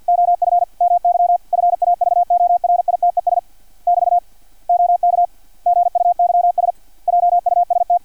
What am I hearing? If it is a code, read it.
Morse code, heard as a keyed tone, about 50 wpm, 700 Hz, with radio noise. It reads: OF MY FAVORITES <BT> OK GUYS JUST
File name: archived / CRSnet